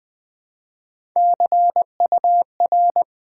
Morse code, heard as a keyed tone, about 20 wpm, 700 Hz. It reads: CUR
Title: Build A Word - 1 Letter At A Time - 3 to 5-Letter Words - 20wpm